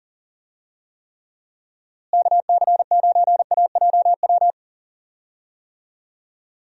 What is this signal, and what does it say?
Morse code, heard as a keyed tone, about 38 wpm, 700 Hz. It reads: KC9AJW
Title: USA Call Signs - 6 Characters - Repeated 3x - 40wpm